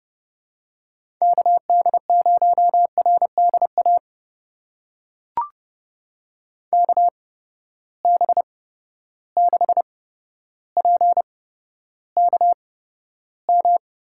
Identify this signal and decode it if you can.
Morse code, heard as a keyed tone, about 30 wpm, spaced as for 13 wpm, 700 Hz. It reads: KD0RDA KB6PKM
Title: USA Call Signs - 8x Character Spacing - 30wpm